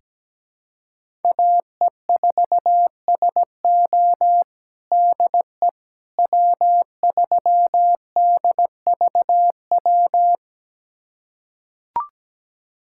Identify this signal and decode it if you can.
Morse code, heard as a keyed tone, about 17 wpm, 700 Hz. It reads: AE4SO DE W3DVW
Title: USA Callsign DE Callsign - 5 Characters - 17wpm